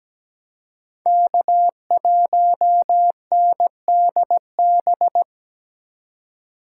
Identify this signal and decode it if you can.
Morse code, heard as a keyed tone, about 17 wpm, 700 Hz. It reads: K1NDB
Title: USA Call Signs - 3x Character Spacing - 17wpm